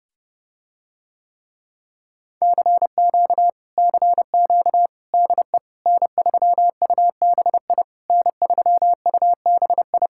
Morse code, heard as a keyed tone, about 30 wpm, 700 Hz. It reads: CQ CQ DE N3UBI N3UBI